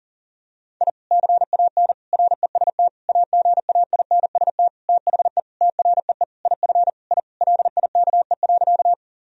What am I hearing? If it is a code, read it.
Morse code, heard as a keyed tone, about 40 wpm, 700 Hz. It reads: I CAN REST AGAINST THE TREE IF I LIKE.
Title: Sentences from Top 1000 Words - Review Part 1 - 40wpm